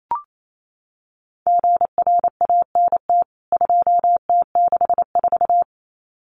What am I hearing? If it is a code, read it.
Morse code, heard as a keyed tone, about 28 wpm, 700 Hz. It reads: GRANT 2T64
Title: CWT Exchange 28wpm